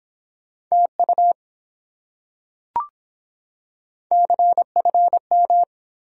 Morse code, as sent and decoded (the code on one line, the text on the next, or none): - ..- / -.-. ..-. --
TU CFM